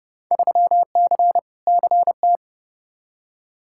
Morse code, heard as a keyed tone, about 32 wpm, 700 Hz. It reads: 3C CT
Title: Field Day Exchanges Encore - 30wpm